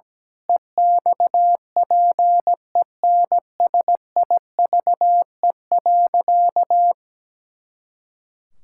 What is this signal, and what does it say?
Morse code, heard as a keyed tone, about 17 wpm, 700 Hz. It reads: EXPENSIVE.
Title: Pangram Challenge v2 - 17wpm